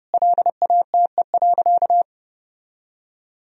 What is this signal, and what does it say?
Morse code, heard as a keyed tone, about 30 wpm, 700 Hz. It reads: LATE.
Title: Sentences from Top 1000 Words - Review Part 4 - 30wpm